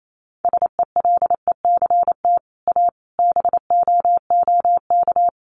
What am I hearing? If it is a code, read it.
Morse code, heard as a keyed tone, about 28 wpm, 700 Hz. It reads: SELECT A BOOK
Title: Sentences from Top 1300 Words 28wpm